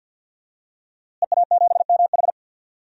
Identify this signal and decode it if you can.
Morse code, heard as a keyed tone, about 53 wpm, 700 Hz. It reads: EA7MH
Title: DXCC-100 Call Signs - 5 Characters - Repeated 2x - 50wpm